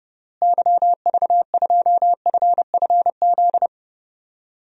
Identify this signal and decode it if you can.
Morse code, heard as a keyed tone, about 30 wpm, 700 Hz. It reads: YV2FFZ
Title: DXCC-100 Call Signs - 6 Characters - Repeated 3x - 30wpm